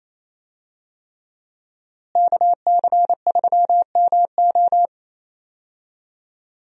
Morse code, heard as a keyed tone, about 28 wpm, 700 Hz. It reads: KC3MO
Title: USA Call Signs - 5 Characters - Repeated 3x - 28wpm